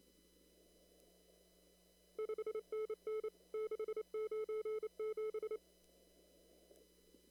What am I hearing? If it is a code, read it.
Morse code, heard as a keyed tone, about 28 wpm, 440 Hz. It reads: 5NN 697
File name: CW / kt5j wpx cw-op-k5pi